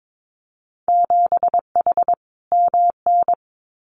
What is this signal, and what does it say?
Morse code, heard as a keyed tone, about 22 wpm, 700 Hz. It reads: 7H MN
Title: Winter Field Day Exchanges Encore 22wpm